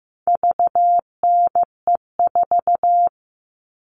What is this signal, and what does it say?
Morse code, heard as a keyed tone, about 15 wpm, 700 Hz. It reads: VNE4
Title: All Cut Numbers - 4-Digit 15wpm